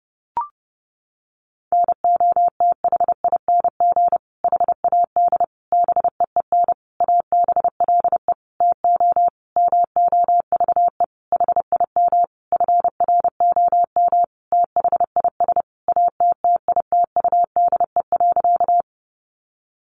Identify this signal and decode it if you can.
Morse code, heard as a keyed tone, about 30 wpm, 700 Hz. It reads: NOTHING HAD BEEN ABLE TO MOVE HIM FROM THIS ATTITUDE.
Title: Sentences from Top 1300 Words 30wpm